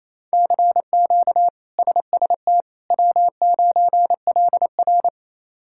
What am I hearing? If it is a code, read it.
Morse code, heard as a keyed tone, about 28 wpm, 700 Hz. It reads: CQ SST W9LR
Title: K1USN SST QSO Practice - 28wpm